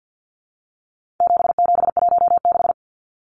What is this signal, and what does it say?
Morse code, heard as a keyed tone, about 50 wpm, 700 Hz. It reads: 7716
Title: Numbers - 4-Digit 50wpm